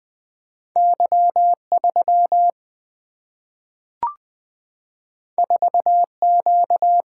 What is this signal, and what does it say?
Morse code, heard as a keyed tone, about 20 wpm, 700 Hz. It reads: Y3 4Q
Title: Single-Letter Single-Number 20wpm